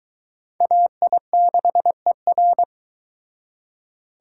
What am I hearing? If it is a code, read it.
Morse code, heard as a keyed tone, about 22 wpm, 700 Hz. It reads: AI6ER